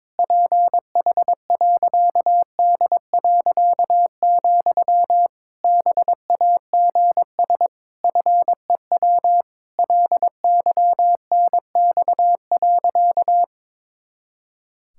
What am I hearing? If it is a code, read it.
Morse code, heard as a keyed tone, about 21 wpm, 700 Hz. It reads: PH.D., BAGS FEW LYNX.
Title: Pangram Challenge v2 - 22wpm